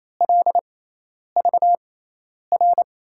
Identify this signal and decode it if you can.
Morse code, heard as a keyed tone, about 28 wpm, spaced as for 10 wpm, 700 Hz. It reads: LVR
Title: USA Call Signs - 6x Character Spacing - 28wpm